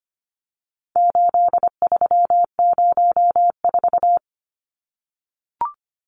Morse code, heard as a keyed tone, about 25 wpm, 700 Hz. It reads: 8304
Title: Numbers - 4-Digit 25wpm